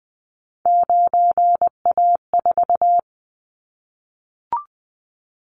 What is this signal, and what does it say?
Morse code, heard as a keyed tone, about 20 wpm, 700 Hz. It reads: 9A4
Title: Callsign Prefixes for Top 50 DXCC - 20wpm